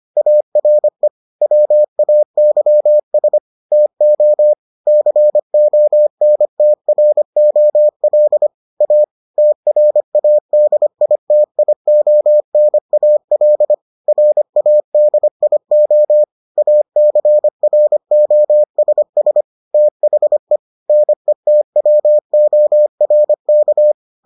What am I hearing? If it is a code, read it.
Morse code, heard as a keyed tone, about 25 wpm, 600 Hz. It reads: ARE WAYS TO CONTROL A TRADITIONAL RADIO ACROSS THE NETWORK